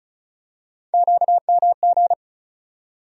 Morse code, heard as a keyed tone, about 36 wpm, 700 Hz. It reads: QMG
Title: USA Call Sign Postfixes - 3 Letters 35wpm